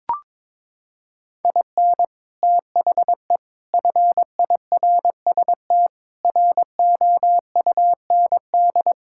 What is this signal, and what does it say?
Morse code, heard as a keyed tone, about 22 wpm, 700 Hz. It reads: IN THE FIRST ROUND
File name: practice / 22wpm / Sets of 6 Words - 22wpm